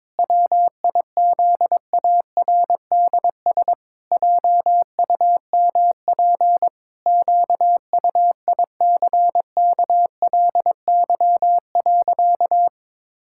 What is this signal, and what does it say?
Morse code, heard as a keyed tone, about 21 wpm, 700 Hz. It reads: WIZARDS JUMP QUICKLY.
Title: Pangram Challenge v2 - 22wpm